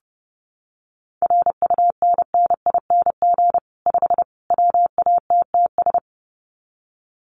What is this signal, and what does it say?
Morse code, heard as a keyed tone, about 30 wpm, 700 Hz. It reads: RUNNING 5 WATTS
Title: Common QSO Phrases 30wpm